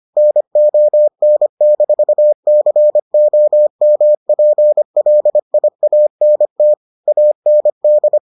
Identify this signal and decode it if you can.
Morse code, heard as a keyed tone, about 25 wpm, 600 Hz. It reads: NON-COMPLIANT AND